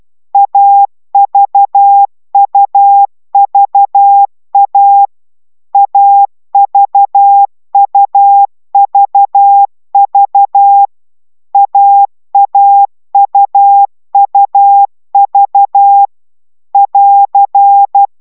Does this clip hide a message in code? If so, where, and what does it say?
Yes: Morse code from 0.3 to 18.1 s, heard as a keyed tone, about 12 wpm, 800 Hz. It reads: AVUVA AVUVV AAUUV <AR>